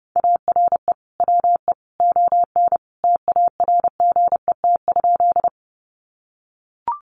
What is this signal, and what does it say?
Morse code, heard as a keyed tone, about 30 wpm, 700 Hz. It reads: ARE WE ON TARGET?
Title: Sentences from Top 1200 Words 30wpm